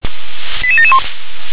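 Five-tone selective call (ZVEI), heard as quick runs of five tones, tones of about 70 ms. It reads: 88771
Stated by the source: ZVEI five-tone selective call received by SDR.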